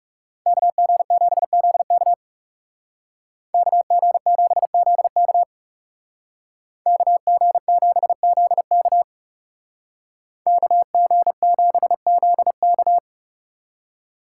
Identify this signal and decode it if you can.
Morse code, heard as a keyed tone, about 36 wpm, 700 Hz. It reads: KG7ZK KG7ZK KG7ZK KG7ZK
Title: Speed Racing - USA 2x2 Call Signs 30wpm